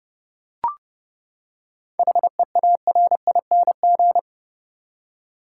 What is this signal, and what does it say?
Morse code, heard as a keyed tone, about 30 wpm, 700 Hz. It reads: HEARING
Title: Seven Letter Words - 30wpm